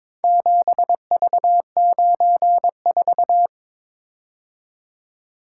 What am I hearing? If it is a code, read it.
Morse code, heard as a keyed tone, about 21 wpm, 700 Hz. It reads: 7V94
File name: practice / 22wpm / Four Letter-Number Combinations - 22wpm